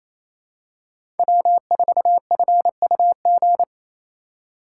Morse code, heard as a keyed tone, about 28 wpm, 700 Hz. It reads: W4FUG
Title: USA Call Signs - 5 Characters - 2x Character Spacing - 28wpm